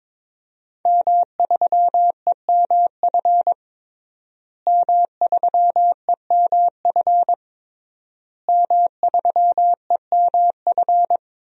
Morse code, heard as a keyed tone, about 22 wpm, 700 Hz. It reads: M3EMF M3EMF M3EMF
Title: DXCC-100 Call Signs - 5 Characters - Repeated 3x - 22wpm